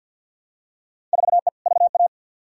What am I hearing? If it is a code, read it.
Morse code, heard as a keyed tone, about 50 wpm, 700 Hz. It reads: 4E VA
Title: Field Day Exchanges Encore - 50wpm